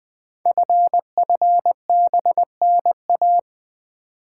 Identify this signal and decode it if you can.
Morse code, heard as a keyed tone, about 20 wpm, 700 Hz. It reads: FFBNA
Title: Code Groups - 5 Letters - 20wpm